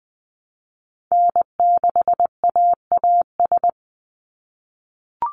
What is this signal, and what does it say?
Morse code, heard as a keyed tone, about 20 wpm, 700 Hz. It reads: N6AAS
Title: USA Call Signs 20wpm